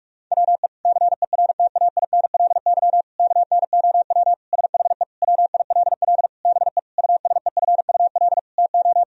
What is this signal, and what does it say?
Morse code, heard as a keyed tone, about 46 wpm, 700 Hz. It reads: WE CERTAINLY KNOW SHE WILL BE USEFUL TO
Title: Sentences from Top 1000 Words - Review Part 4 - 45wpm